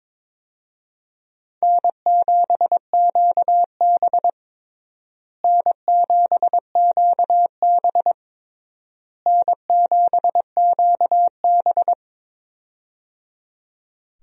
Morse code, heard as a keyed tone, about 22 wpm, 700 Hz. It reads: N7QB N7QB N7QB
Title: USA Call Signs - 4 Characters - Repeated 3x - 22wpm